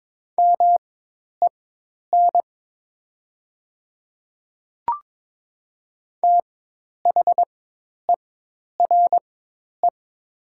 Morse code, heard as a keyed tone, about 22 wpm, spaced as for 10 wpm, 700 Hz. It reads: MEN THERE
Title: 4x Character Spacing - Top 200 Words - 22wpm